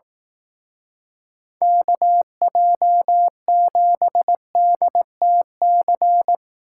Morse code, heard as a keyed tone, about 18 wpm, 700 Hz. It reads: KJ7DTC
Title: Speed Racing - USA 2x3 Call Signs 15wpm